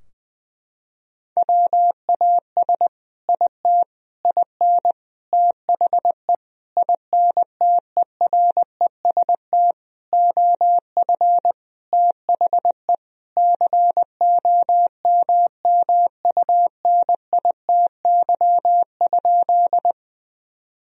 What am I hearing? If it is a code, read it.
Morse code, heard as a keyed tone, about 20 wpm, 700 Hz. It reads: WAS IT IN THE INTEREST OF THE COMMUNITY?
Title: Sentences from Top 1000 Words - Review Part 4 - 20wpm